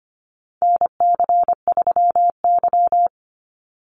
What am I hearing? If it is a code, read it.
Morse code, heard as a keyed tone, about 25 wpm, 700 Hz. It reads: NC3Y
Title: Call Signs Repeated 2x - 25wpm